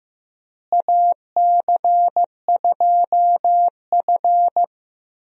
Morse code, heard as a keyed tone, about 15 wpm, 700 Hz. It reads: AC2F
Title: USA Call Signs - 4 Characters - Repeated 3x - 15wpm